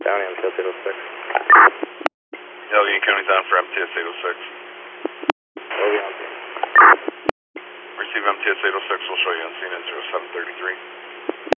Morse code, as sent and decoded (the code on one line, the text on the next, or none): none